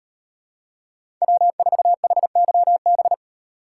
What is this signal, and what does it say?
Morse code, heard as a keyed tone, about 38 wpm, 700 Hz. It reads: W4HYB